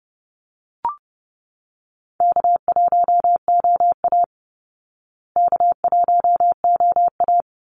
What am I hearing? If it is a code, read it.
Morse code, heard as a keyed tone, about 30 wpm, 700 Hz. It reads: K1OA K1OA
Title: Easy Call Signs Repeated 3x - 30wpm